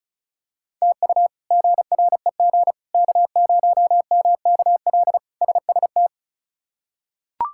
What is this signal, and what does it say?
Morse code, heard as a keyed tone, about 36 wpm, 700 Hz. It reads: TU GREG K0MKL SST
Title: K1USN SST QSO Practice - 35wpm